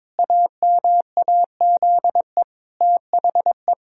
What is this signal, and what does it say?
Morse code, heard as a keyed tone, about 22 wpm, 700 Hz. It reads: AMAZE THE
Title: Pangram Challenge v2 - 22wpm